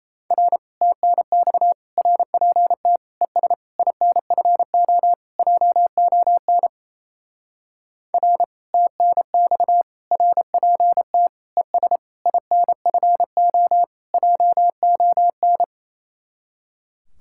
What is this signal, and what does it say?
Morse code, heard as a keyed tone, about 31 wpm, 700 Hz. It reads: R TNX RPT ES INFO JON R TNX RPT ES INFO JON
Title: Speed-Racing Common QSO Phrases - 28wpm